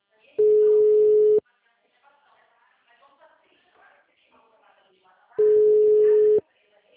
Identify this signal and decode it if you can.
Ringback tone (Europe, 425 Hz), with speech.
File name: out- -2041- - - .14973